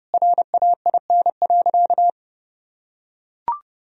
Morse code, heard as a keyed tone, about 29 wpm, 700 Hz. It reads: RAIN.
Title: Sentences from Top 1000 Words - Review Part 3 - 30wpm